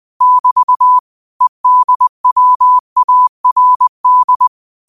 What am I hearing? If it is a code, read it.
Morse code, heard as a keyed tone, about 20 wpm, 1000 Hz. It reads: <BT> EDWARD
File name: qotdfiles / 20wpm